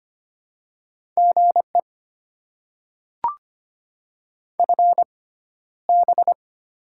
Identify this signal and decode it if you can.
Morse code, heard as a keyed tone, about 25 wpm, spaced as for 13 wpm, 700 Hz. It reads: GE FB
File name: Top 20 QSO Elements - 6x Character Spacing - 25wpm